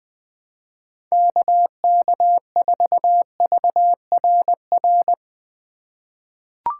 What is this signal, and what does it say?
Morse code, heard as a keyed tone, about 21 wpm, 700 Hz. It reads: KK4VRR
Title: USA Call Signs - 6 Characters - 2x Character Spacing - 20wpm